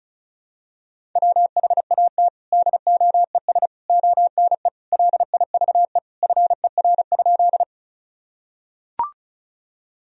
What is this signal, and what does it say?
Morse code, heard as a keyed tone, about 34 wpm, 700 Hz. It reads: WHAT DOES ONE LIVE FER?
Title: Sentences from Top 200 Words and Abbreviations 35wpm